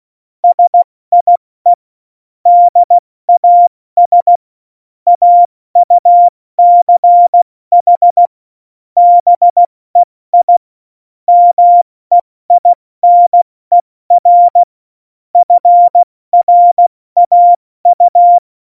Morse code, heard as a keyed tone, about 16 wpm, 700 Hz. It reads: SIE DAS AUCH BEI MEINER FRAU
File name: Morsekurs / Witze / Witz 13 March 2018, 70 F80